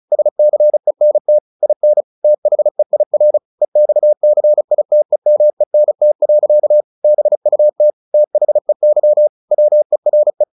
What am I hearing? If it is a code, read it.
Morse code, heard as a keyed tone, about 34 wpm, 600 Hz. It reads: SCENT IN THEIR EXCITEMENT. BUT THEY WERE
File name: the old man and the sea 2 35wpm